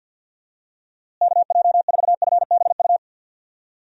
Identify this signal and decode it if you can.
Morse code, heard as a keyed tone, about 53 wpm, 700 Hz. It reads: KJ4FBU
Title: USA Call Signs - 6 Characters - Repeated 2x - 50wpm